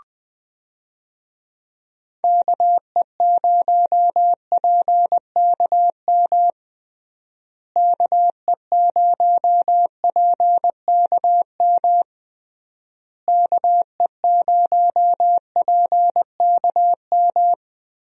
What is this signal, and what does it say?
Morse code, heard as a keyed tone, about 21 wpm, 700 Hz. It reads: KE0PKM KE0PKM KE0PKM
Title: USA Call Signs - 6 Characters - Repeated 3x - 20wpm